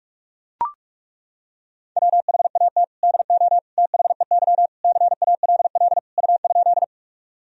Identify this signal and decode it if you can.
Morse code, heard as a keyed tone, about 42 wpm, 700 Hz. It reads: WHAT DO THEY CALL U?
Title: Sentences from Top 100 Words and Abbreviations 45wpm